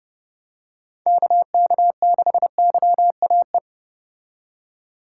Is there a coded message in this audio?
Yes: Morse code from 1.1 to 3.6 s, heard as a keyed tone, about 29 wpm, 700 Hz. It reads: KK6YAE